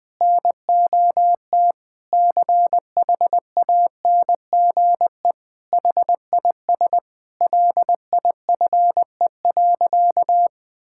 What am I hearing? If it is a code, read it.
Morse code, heard as a keyed tone, about 20 wpm, 700 Hz. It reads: NOT CHANGE HIS LIFE.
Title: Sentences from Top 1000 Words - Review Part 5 - 20wpm